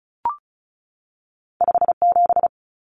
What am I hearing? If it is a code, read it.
Morse code, heard as a keyed tone, about 36 wpm, 700 Hz. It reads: 57
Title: Numbers - 2-Digit 35wpm